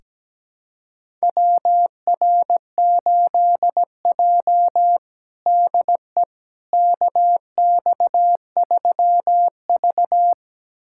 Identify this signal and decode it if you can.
Morse code, heard as a keyed tone, about 17 wpm, 700 Hz. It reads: WR8J DE KX3V
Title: USA Callsign DE Callsign - 4 Characters - 17wpm